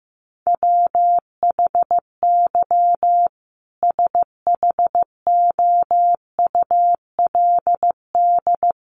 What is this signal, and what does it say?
Morse code, heard as a keyed tone, about 15 wpm, 700 Hz. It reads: WHY SHOULD
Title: Sentences from Top 300 Words 15wpm